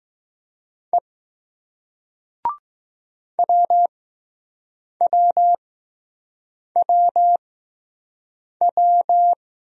Morse code, heard as a keyed tone, about 20 wpm, spaced as for 6 wpm, 700 Hz. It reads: E WWWW